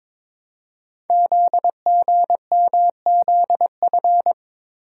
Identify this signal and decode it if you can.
Morse code, heard as a keyed tone, about 22 wpm, 700 Hz. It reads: ZGMZF
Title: Code Groups - 5 Letters - 22wpm